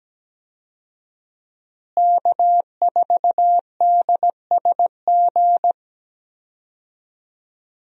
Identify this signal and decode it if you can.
Morse code, heard as a keyed tone, about 17 wpm, 700 Hz. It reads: K4DSG
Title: USA Call Signs - 5 Characters - Repeated 2x - 17wpm